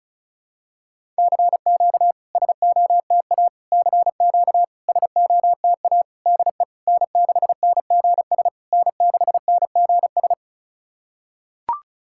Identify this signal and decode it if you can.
Morse code, heard as a keyed tone, about 36 wpm, 690 Hz. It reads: CQ SOTA CQ SOTA DE N6NGS N6NGS
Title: SOTA QSO Practice 35wpm